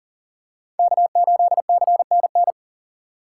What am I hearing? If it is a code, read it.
Morse code, heard as a keyed tone, about 37 wpm, 700 Hz. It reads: K8CNN
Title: USA Call Signs - 6x Character Spacing - 40wpm